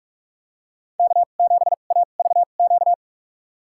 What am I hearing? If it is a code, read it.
Morse code, heard as a keyed tone, about 46 wpm, spaced as for 33 wpm, 700 Hz. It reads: K7AVQ
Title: USA Call Signs - 5 Characters - 2x Character Spacing - 45wpm